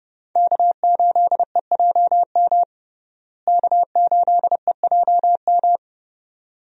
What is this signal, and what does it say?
Morse code, heard as a keyed tone, about 31 wpm, 700 Hz. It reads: K8EJM K8EJM
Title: USA Call Signs - 5 Characters - Repeated 2x - 30wpm